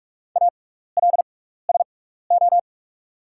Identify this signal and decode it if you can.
Morse code, heard as a keyed tone, about 47 wpm, spaced as for 16 wpm, 690 Hz. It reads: ALSO
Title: 6x Character Spacing - Top 200 Words - 45wpm